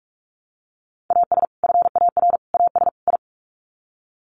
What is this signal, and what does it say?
Morse code, heard as a keyed tone, about 44 wpm, 700 Hz. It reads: AS FAR AS I
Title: Sets of 4 Words 45wpm